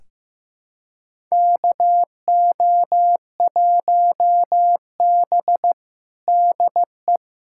Morse code, heard as a keyed tone, about 15 wpm, 700 Hz. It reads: KO1B DE
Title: USA Callsign DE Callsign - 4 Characters - 15wpm